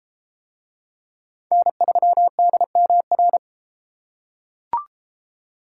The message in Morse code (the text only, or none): N3DMR